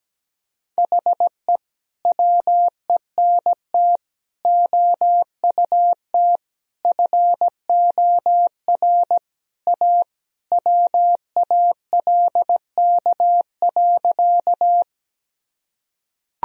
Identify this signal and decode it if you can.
Morse code, heard as a keyed tone, about 17 wpm, 700 Hz. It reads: HE WENT OUT FOR A WALK.
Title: Sentences from Top 1000 Words - Review Part 1 - 17wpm